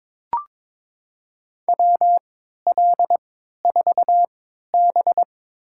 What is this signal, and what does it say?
Morse code, heard as a keyed tone, about 22 wpm, spaced as for 13 wpm, 700 Hz. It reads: WL4B
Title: USA Call Signs - 4 Characters - 3x Character Spacing - 22wpm